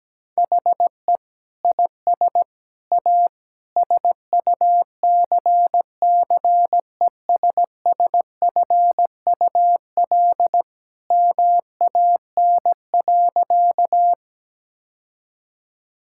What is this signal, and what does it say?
Morse code, heard as a keyed tone, about 17 wpm, 700 Hz. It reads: HE IS A SUCCESSFUL MAN.
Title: Sentences from Top 1000 Words - Review Part 6 - 17wpm